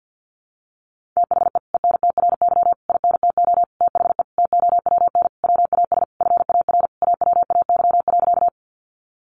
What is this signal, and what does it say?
Morse code, heard as a keyed tone, about 50 wpm, 700 Hz. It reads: THE ENTRY INTO THE TOWN WAS FAR AWAY.
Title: Sentences from Top 1500 Words 50wpm